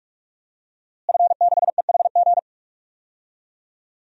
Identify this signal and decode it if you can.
Morse code, heard as a keyed tone, about 46 wpm, 700 Hz. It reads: F6EHG
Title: DXCC-100 Call Signs - 5 Characters - Repeated 3x - 45wpm